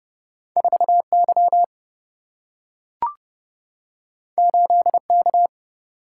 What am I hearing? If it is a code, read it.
Morse code, heard as a keyed tone, about 31 wpm, 700 Hz. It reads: 4Y 8K
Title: Single-Letter Single-Number 30wpm